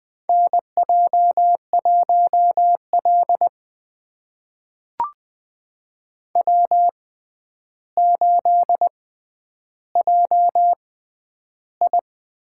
Morse code, heard as a keyed tone, about 20 wpm, spaced as for 10 wpm, 700 Hz. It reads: NJ1L W8JI
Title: USA Call Signs - 4 Characters - 6x Character Spacing - 20wpm